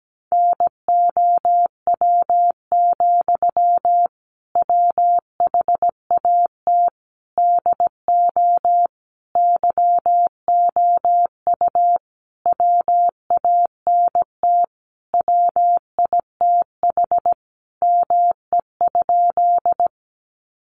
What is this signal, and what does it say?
Morse code, heard as a keyed tone, about 17 wpm, 700 Hz. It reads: NOW, WHAT DO YOU WANT WITH ME?
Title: Sentences from Top 200 Words 17wpm